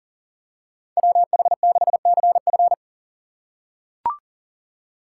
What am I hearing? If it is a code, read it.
Morse code, heard as a keyed tone, about 37 wpm, 700 Hz. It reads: WH6CF
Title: USA Call Signs - 4x Character Spacing - 40wpm